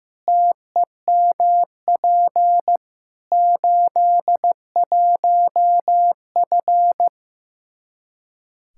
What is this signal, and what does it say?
Morse code, heard as a keyed tone, about 15 wpm, 700 Hz. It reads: TEMP 81F
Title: Practice QSO - Exchange 2 - Info - 15wpm